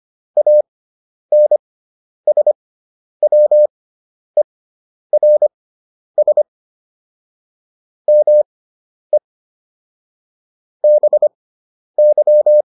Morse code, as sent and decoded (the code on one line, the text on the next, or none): .- -. ... .-- . .-. ... / -- . / -... -.--
ANSWERS ME BY